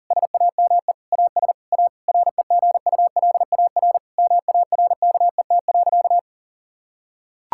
Morse code, heard as a keyed tone, about 40 wpm, 700 Hz. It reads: SAME AS A REGULAR MARKET.
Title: Sentences from Top 1000 Words - Review Part 6 - 40wpm